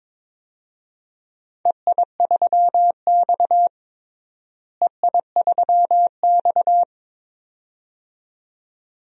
Morse code, heard as a keyed tone, about 22 wpm, 700 Hz. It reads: EI3X EI3X
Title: DXCC-100 Call Signs - 4 Characters - Repeated 2x - 22wpm